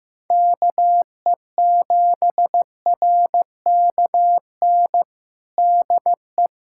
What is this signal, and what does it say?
Morse code, heard as a keyed tone, about 15 wpm, 700 Hz. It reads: KE7RKN DE
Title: USA Callsign DE Callsign 15wpm